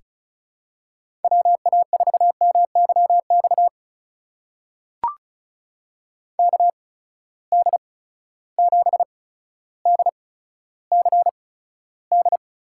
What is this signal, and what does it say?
Morse code, heard as a keyed tone, about 36 wpm, spaced as for 15 wpm, 700 Hz. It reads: WA4MYX KD7DCD
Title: USA Call Signs - 6 Characters - 8x Character Spacing - 35wpm